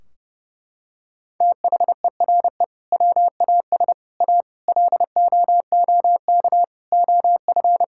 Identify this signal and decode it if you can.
Morse code, heard as a keyed tone, about 30 wpm, 700 Hz. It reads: THERE WAS A LOOK OF
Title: Sentences from Top 1600 Words 30wpm